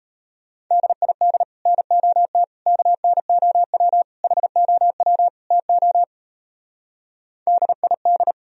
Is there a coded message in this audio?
Yes: Morse code from 0.7 to 8.3 s, heard as a keyed tone, about 37 wpm, 700 Hz. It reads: DID NOT KNOW HOW TO DID